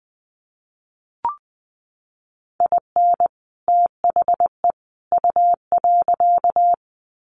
Morse code, heard as a keyed tone, about 20 wpm, 700 Hz. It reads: IN THE U.
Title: Sets of 3 Words 20wpm